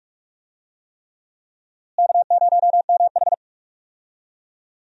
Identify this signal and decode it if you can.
Morse code, heard as a keyed tone, about 46 wpm, 700 Hz. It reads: K0MH
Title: USA Call Signs - 4 Characters - Repeated 3x - 45wpm